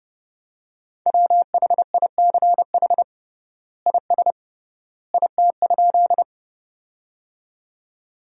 Morse code, heard as a keyed tone, about 32 wpm, 700 Hz. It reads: WHICH IS IT?
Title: 3x Word-Spacing - Sentences from Top 100 Words - 30wpm